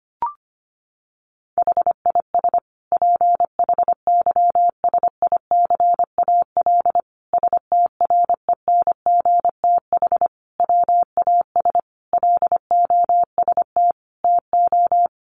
Morse code, heard as a keyed tone, about 25 wpm, 700 Hz. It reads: HIS PHYSICAL STRENGTH WAS LOST TO
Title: Sentences from Top 1400 Words 25wpm